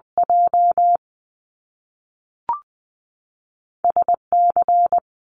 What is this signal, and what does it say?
Morse code, heard as a keyed tone, about 20 wpm, 700 Hz. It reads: J S C
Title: Top Two Letter Combinations in Words 20wpm